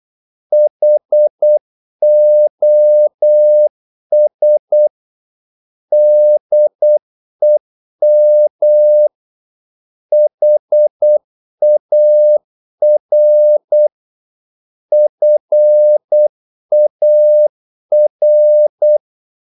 Morse code, heard as a keyed tone, about 8 wpm, 600 Hz. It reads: HOS DEM HAR FAR
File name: Korte ord DK 8wpm